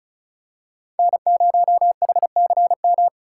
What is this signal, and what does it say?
Morse code, heard as a keyed tone, about 36 wpm, 700 Hz. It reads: N0HCM
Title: USA Call Signs - 5 Characters - 2x Character Spacing - 35wpm